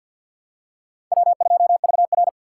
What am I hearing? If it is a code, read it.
Morse code, heard as a keyed tone, about 46 wpm, 700 Hz. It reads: W2VR
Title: USA Call Signs - 4 Characters - 2x Character Spacing - 50wpm